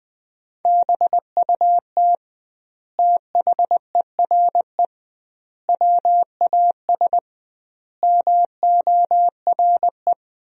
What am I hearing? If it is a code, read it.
Morse code, heard as a keyed tone, about 21 wpm, 700 Hz. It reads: BUT THERE WAS MORE
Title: 2x Word-Spacing - Sentences from Top 100 Words - 20wpm